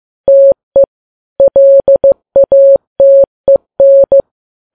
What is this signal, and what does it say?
Morse code, heard as a keyed tone, about 15 wpm, 550 Hz. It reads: TE LATEN
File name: Files Web tls / Morse on9fv / morse15kps